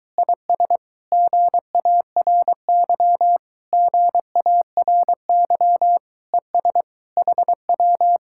Morse code, heard as a keyed tone, about 22 wpm, 700 Hz. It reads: IS GARY GARY ES HW